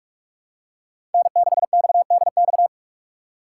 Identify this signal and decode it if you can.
Morse code, heard as a keyed tone, about 46 wpm, 700 Hz. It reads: N6XDX
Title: USA Call Signs - 5 Characters - 4x Character Spacing - 45wpm